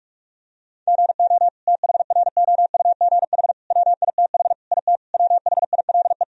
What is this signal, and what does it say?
Morse code, heard as a keyed tone, about 47 wpm, 700 Hz. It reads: GO THROUGH WITH IT WHILE
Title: Sentences from Top 1000 Words - Review Part 1 - 45wpm